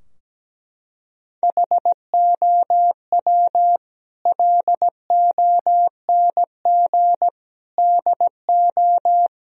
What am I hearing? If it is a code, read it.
Morse code, heard as a keyed tone, about 17 wpm, 700 Hz. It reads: HOW LONG DO
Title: Sentences from Top 1000 Words - Review Part 2 - 17wpm